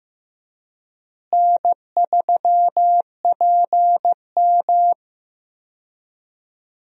Morse code, heard as a keyed tone, about 15 wpm, 700 Hz. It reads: N3PM
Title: USA Call Signs - 4 Characters - Repeated 2x - 15wpm